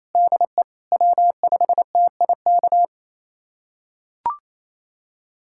Morse code, heard as a keyed tone, about 28 wpm, 700 Hz. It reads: DE W5TIK
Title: USA Callsign DE Callsign - 5 Characters - 28wpm